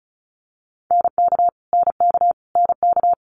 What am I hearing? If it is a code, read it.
Morse code, heard as a keyed tone, about 36 wpm, 700 Hz. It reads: NK NK NK
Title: Multi-letter Phonemes Repeated 3x 35wpm